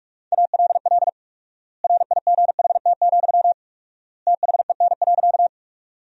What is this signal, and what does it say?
Morse code, heard as a keyed tone, about 42 wpm, 700 Hz. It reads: ALL RIGHT, THEN.
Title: 4x Word-Spacing - Sentences from Top 200 Words - 45wpm